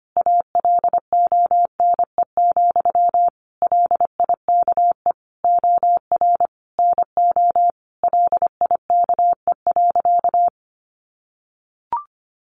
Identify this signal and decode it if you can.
Morse code, heard as a keyed tone, about 25 wpm, 700 Hz. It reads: ALONE, LIKE OR NO LIKE.
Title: Sentences from Top 1000 Words 25wpm